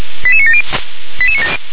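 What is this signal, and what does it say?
Five-tone selective call (ZVEI), heard as quick runs of five tones, tones of about 70 ms. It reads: 88970
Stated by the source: ZVEI five-tone selective call received on SDR.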